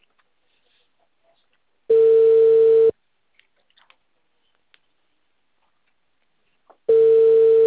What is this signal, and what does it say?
Ringback tone (China, 450 Hz).